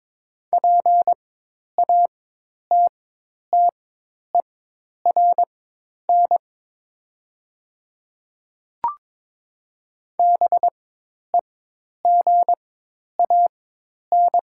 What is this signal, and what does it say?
Morse code, heard as a keyed tone, about 22 wpm, spaced as for 10 wpm, 700 Hz. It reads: PATTERN BEGAN
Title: 4x Character Spacing - Top 500 Words - 22wpm